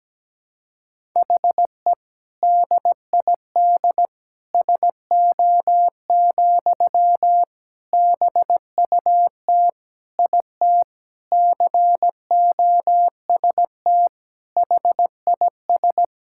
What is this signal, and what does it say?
Morse code, heard as a keyed tone, about 17 wpm, 700 Hz. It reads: HE DID SO, BUT IT COST HIS
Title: Sentences from Top 1000 Words - Review Part 5 - 17wpm